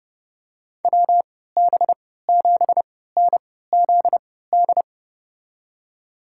Morse code, heard as a keyed tone, about 30 wpm, spaced as for 17 wpm, 700 Hz. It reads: WB7NZD
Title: USA Call Signs - 6 Characters - 3x Character Spacing - 30wpm